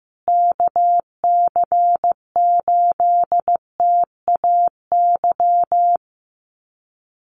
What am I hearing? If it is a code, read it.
Morse code, heard as a keyed tone, about 15 wpm, 700 Hz. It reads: KC8TAY
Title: USA Call Signs Repeated 2x 15wpm